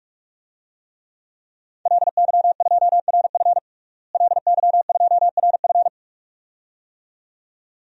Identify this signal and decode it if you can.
Morse code, heard as a keyed tone, about 48 wpm, 700 Hz. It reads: LY2RF LY2RF
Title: DXCC-100 Call Signs - 5 Characters - Repeated 2x - 45wpm